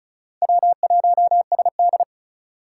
Morse code, heard as a keyed tone, about 36 wpm, 700 Hz. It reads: W1SD
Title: USA Call Signs - 4 Characters - 3x Character Spacing - 35wpm